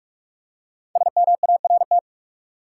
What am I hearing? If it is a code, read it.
Morse code, heard as a keyed tone, about 44 wpm, 700 Hz. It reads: SMART
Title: Five Letter Words - 45wpm